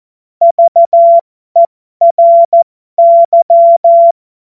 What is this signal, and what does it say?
Morse code, heard as a keyed tone, about 14 wpm, spaced as for 12 wpm, 680 Hz. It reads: VERY
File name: Morsekurs / Witze